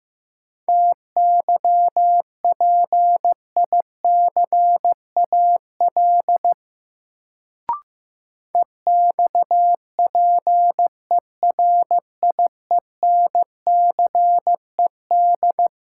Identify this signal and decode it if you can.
Morse code, heard as a keyed tone, about 15 wpm, 700 Hz. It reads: TYPICAL EXPERIENCED
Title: Sentences from Top 1100 Words - New Word Drill - 15wpm